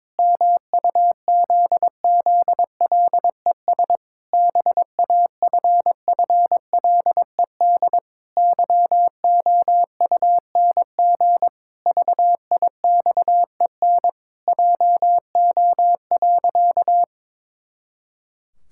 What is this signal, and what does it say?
Morse code, heard as a keyed tone, about 22 wpm, 700 Hz. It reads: MUZZLES BAFFLED YOUNG VIXEN JO.
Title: Pangram Challenge v2 - 22wpm